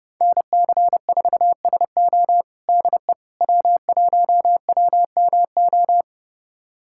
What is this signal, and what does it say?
Morse code, heard as a keyed tone, about 32 wpm, 700 Hz. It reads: NC4SO DE W1WMO
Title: USA Callsign DE Callsign - 5 Characters - 30wpm